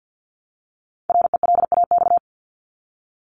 Morse code, heard as a keyed tone, about 50 wpm, 700 Hz. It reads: RELAX
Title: Top 1251-1500 Words 50wpm